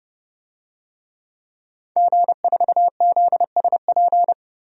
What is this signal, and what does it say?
Morse code, heard as a keyed tone, about 30 wpm, 700 Hz. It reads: G4ZSP
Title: DXCC-100 Call Signs - 5 Characters - 30wpm